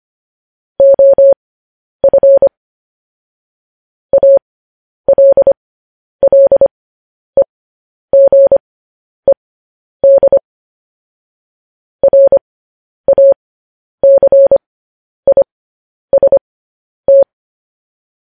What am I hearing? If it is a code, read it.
Morse code, heard as a keyed tone, about 25 wpm, spaced as for 10 wpm, 550 Hz. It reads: OF ALLEGED RACIST